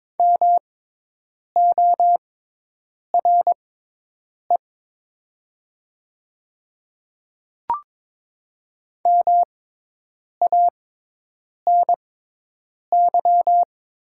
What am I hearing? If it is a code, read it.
Morse code, heard as a keyed tone, about 22 wpm, spaced as for 8 wpm, 700 Hz. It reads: MORE MANY
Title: 6x Character Spacing - Top 100 Words - 22wpm